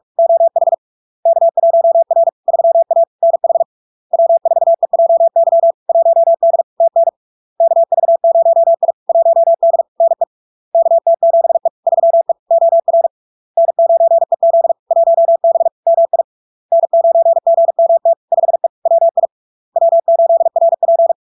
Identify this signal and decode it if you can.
Morse code, heard as a keyed tone, about 46 wpm, 670 Hz. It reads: OH K1R 3A NH W4EJY 1D TN KV0I 1D NE KT7E 3E OR N9EZ 1B MI N9GMT 5E WI W8RP
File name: LARK-FD-2022-Log-in-CW-45wpm-670-Hz